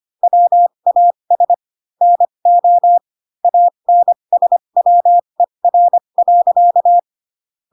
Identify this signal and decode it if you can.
Morse code, heard as a keyed tone, about 25 wpm, spaced as for 22 wpm, 700 Hz. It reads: WAS NO ANSWER.
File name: Pinocchio / Pinocchio ch3 18 25 ik1hge